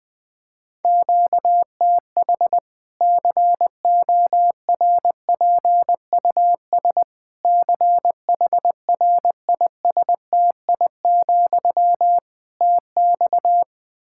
Morse code, heard as a keyed tone, about 20 wpm, 700 Hz. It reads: QTH CORPUS CHRISTI, TX
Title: Practice QSO - Exchange 1 - Beginning - 20wpm